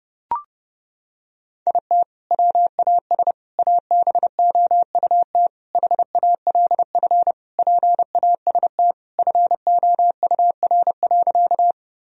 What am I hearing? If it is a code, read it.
Morse code, heard as a keyed tone, about 32 wpm, 700 Hz. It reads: IT WAS ABOUT HALF PAST FOUR.
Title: Sentences from Top 1000 Words - Review Part 3 - 30wpm